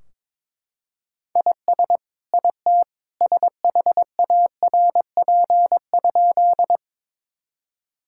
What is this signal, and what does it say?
Morse code, heard as a keyed tone, about 22 wpm, 700 Hz. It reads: IS IT SHARP?
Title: Sentences from Top 1600 Words 22wpm